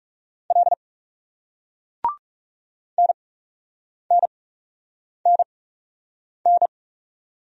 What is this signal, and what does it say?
Morse code, heard as a keyed tone, about 40 wpm, spaced as for 8 wpm, 700 Hz. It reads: L NNNN